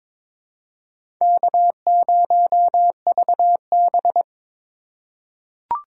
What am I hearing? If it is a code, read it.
Morse code, heard as a keyed tone, about 23 wpm, 700 Hz. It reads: K0VB
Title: USA Call Signs - 4 Characters - 3x Character Spacing - 22wpm